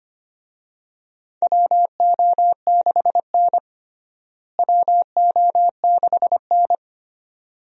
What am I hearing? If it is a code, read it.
Morse code, heard as a keyed tone, about 25 wpm, 700 Hz. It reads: WO6N WO6N
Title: USA Call Signs - 4 Characters - Repeated 2x - 25wpm